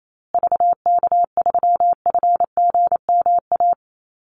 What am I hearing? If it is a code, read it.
Morse code, heard as a keyed tone, about 28 wpm, 700 Hz. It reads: VK3FGMA
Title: USA Call Signs Repeated 2x 28wpm